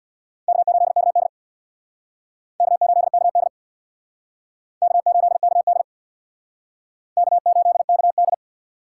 Morse code, heard as a keyed tone, about 64 wpm, 700 Hz. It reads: K8KD K8KD K8KD K8KD